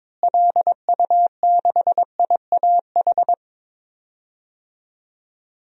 Morse code, heard as a keyed tone, about 22 wpm, 700 Hz. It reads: LU6IAH
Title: DXCC-100 Call Signs - 6 Characters - Repeated 2x - 22wpm